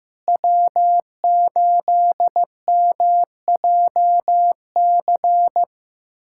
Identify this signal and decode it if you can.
Morse code, heard as a keyed tone, about 15 wpm, 700 Hz. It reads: W8MJC
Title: USA Callsign DE Callsign - 5 Characters - 15wpm